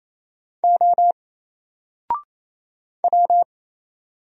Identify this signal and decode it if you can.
Morse code, heard as a keyed tone, about 27 wpm, 700 Hz. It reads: O W
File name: Single-Letter-Number-28wpm